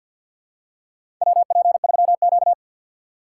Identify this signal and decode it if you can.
Morse code, heard as a keyed tone, about 48 wpm, 700 Hz. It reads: WP3Q
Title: USA Call Signs - 4 Characters - 6x Character Spacing - 50wpm